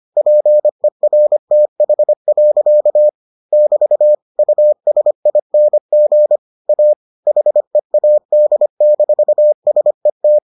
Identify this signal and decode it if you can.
Morse code, heard as a keyed tone, about 25 wpm, 600 Hz. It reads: PERTH. <BT> USING A HEAD-SET